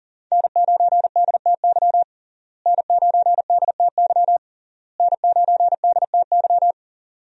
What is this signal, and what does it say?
Morse code, heard as a keyed tone, about 37 wpm, 700 Hz. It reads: N9DTY N9DTY N9DTY
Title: USA Call Signs - 5 Characters - Repeated 3x - 40wpm